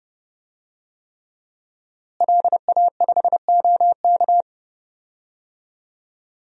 Morse code, heard as a keyed tone, about 32 wpm, 700 Hz. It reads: LA5OK
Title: DXCC-100 Call Signs - 5 Characters - Repeated 3x - 30wpm